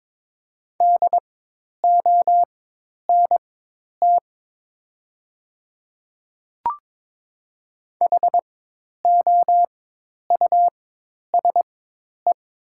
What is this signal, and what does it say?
Morse code, heard as a keyed tone, about 21 wpm, spaced as for 10 wpm, 700 Hz. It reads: DONT HOUSE
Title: 4x Character Spacing - Top 500 Words - 22wpm